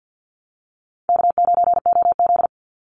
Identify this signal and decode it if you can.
Morse code, heard as a keyed tone, about 53 wpm, 700 Hz. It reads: K9OZ
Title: Call Signs Repeated 3x - 50wpm